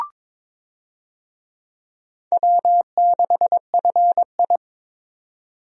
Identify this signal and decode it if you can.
Morse code, heard as a keyed tone, about 22 wpm, 700 Hz. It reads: W6FI
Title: USA Call Signs - 4 Characters - Repeated 3x - 22wpm